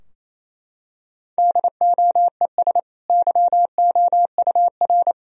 Morse code, heard as a keyed tone, about 27 wpm, 690 Hz. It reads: DOES YOUR
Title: Sentences from Top 1000 Words - Review Part 3 - 28wpm